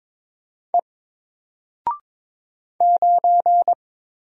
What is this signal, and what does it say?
Morse code, heard as a keyed tone, about 22 wpm, 700 Hz. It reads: E 9
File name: Single-Letter-Number-22wpm